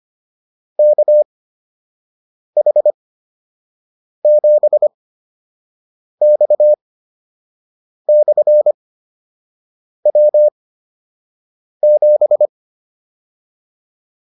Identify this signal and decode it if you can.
Morse code, heard as a keyed tone, about 25 wpm, spaced as for 6 wpm, 600 Hz. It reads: KH7X/W7